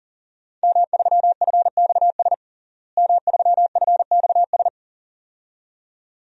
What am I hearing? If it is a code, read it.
Morse code, heard as a keyed tone, about 40 wpm, 700 Hz. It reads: M3FXS M3FXS
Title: DXCC-100 Call Signs - 5 Characters - Repeated 2x - 40wpm